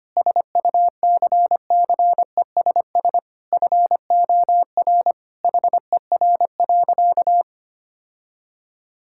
Morse code, heard as a keyed tone, about 25 wpm, 700 Hz. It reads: SUCCESS FOR HER.
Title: Sentences from Top 1000 Words - Review Part 6 - 25wpm